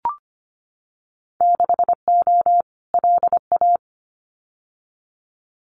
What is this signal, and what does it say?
Morse code, heard as a keyed tone, about 25 wpm, 700 Hz. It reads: 6O LA
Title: Winter Field Day Exchanges Encore 25wpm